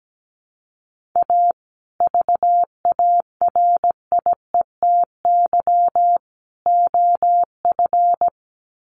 Morse code, heard as a keyed tone, about 17 wpm, 700 Hz. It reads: A VARIETY OF
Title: Sets of 3 Words 17wpm